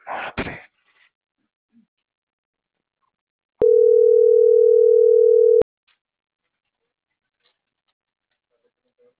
Ringback tone (North America), 440 + 480 Hz.